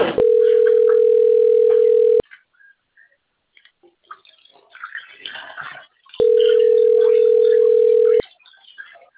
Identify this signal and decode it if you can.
Ringback tone (North America), 440 + 480 Hz, with speech.
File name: rg-1600-unknown- - - .38523